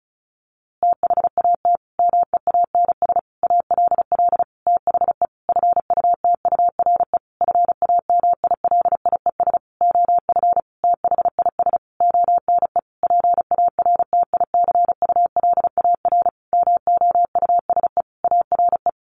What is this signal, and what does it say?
Morse code, heard as a keyed tone, about 34 wpm, 700 Hz. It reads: THAT MEANS ALL THE FUTURE FAMILIES OF THIS ONE PARTICULAR MOUSE ARE
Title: A Sound Of Thunder 35wpm